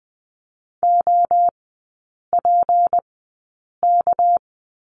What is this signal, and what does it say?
Morse code, heard as a keyed tone, about 20 wpm, spaced as for 8 wpm, 700 Hz. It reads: OPK
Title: 205 Lesson 33e - Letter j - Three Characters 20wpm